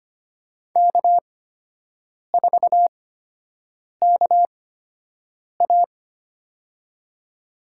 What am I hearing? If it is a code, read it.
Morse code, heard as a keyed tone, about 25 wpm, spaced as for 7 wpm, 700 Hz. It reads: K4KA
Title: USA Call Signs - 4 Characters - 8x Character Spacing - 25wpm